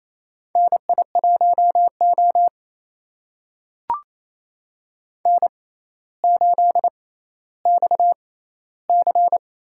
Morse code, heard as a keyed tone, about 28 wpm, spaced as for 14 wpm, 700 Hz. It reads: NI1O N8XC
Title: USA Call Signs - 4 Characters - 6x Character Spacing - 28wpm